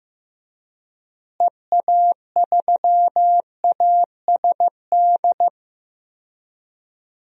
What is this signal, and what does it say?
Morse code, heard as a keyed tone, about 15 wpm, 700 Hz. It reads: EA3ASD
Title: DXCC-100 Call Signs - 6 Characters - Repeated 3x - 15wpm